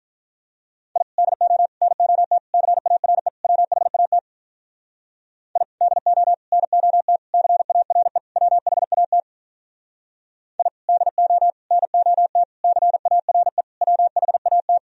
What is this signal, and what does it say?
Morse code, heard as a keyed tone, about 44 wpm, 700 Hz. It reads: I DO NOT CARE WHAT I DO NOT CARE WHAT I DO NOT CARE WHAT